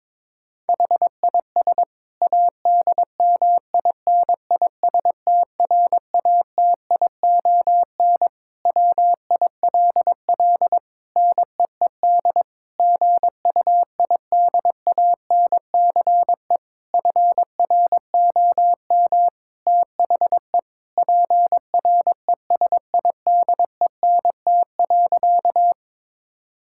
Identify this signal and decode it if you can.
Morse code, heard as a keyed tone, about 22 wpm, 700 Hz. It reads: HIS ADMINISTRATION WILL NEED GUIDANCE FROM THE PRESIDENT.
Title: Sentences from Top 1600 Words 22wpm